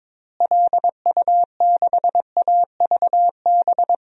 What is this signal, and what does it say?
Morse code, heard as a keyed tone, about 22 wpm, 700 Hz. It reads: LU6AVB
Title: DXCC-100 Call Signs - 6 Characters - Repeated 3x - 22wpm